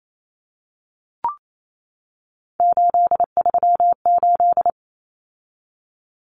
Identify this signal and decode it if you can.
Morse code, heard as a keyed tone, about 28 wpm, 700 Hz. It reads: 838
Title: Numbers - 3-Digit 28wpm